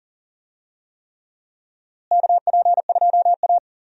Morse code, heard as a keyed tone, about 40 wpm, 700 Hz. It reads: KP2A
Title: DXCC-100 Call Signs - 4 Characters - Repeated 3x - 40wpm